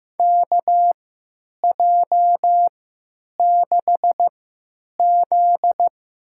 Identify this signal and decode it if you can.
Morse code, heard as a keyed tone, about 15 wpm, spaced as for 9 wpm, 700 Hz. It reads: KJ6Z
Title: USA Call Signs - 3x Character Spacing - 15wpm